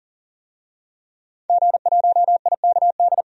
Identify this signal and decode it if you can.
Morse code, heard as a keyed tone, about 40 wpm, 700 Hz. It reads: G1IKD
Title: DXCC-100 Call Signs - 5 Characters - 40wpm